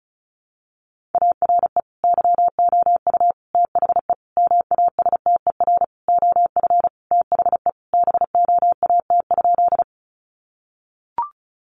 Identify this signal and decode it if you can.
Morse code, heard as a keyed tone, about 34 wpm, 700 Hz. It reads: ARE YOU THE MASTER OF THE BOAT?
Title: Sentences from Top 1300 Words 35wpm